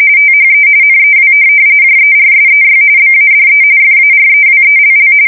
RTTY, about 45.45 baud. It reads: THE NEW JERSEY BADE FAREWELL
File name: fsk170 / USS-New-Jersey